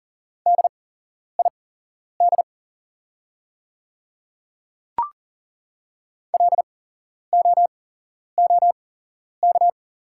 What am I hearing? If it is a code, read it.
Morse code, heard as a keyed tone, about 40 wpm, spaced as for 11 wpm, 700 Hz. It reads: DID LOOK
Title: 8x Character Spacing - Top 300 Words - 40wpm